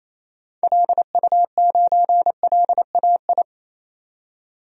Morse code, heard as a keyed tone, about 27 wpm, 700 Hz. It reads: LU9LAI
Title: DXCC-100 Call Signs - 6 Characters - 28wpm